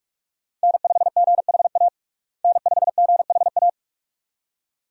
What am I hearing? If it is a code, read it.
Morse code, heard as a keyed tone, about 41 wpm, 700 Hz. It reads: N5GHA N5GHA
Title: USA Call Signs - 5 Characters - Repeated 2x - 45wpm